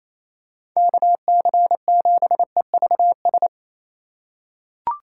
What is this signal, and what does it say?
Morse code, heard as a keyed tone, about 27 wpm, 700 Hz. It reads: KC7EVS
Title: USA Call Signs - 6 Characters - 3x Character Spacing - 28wpm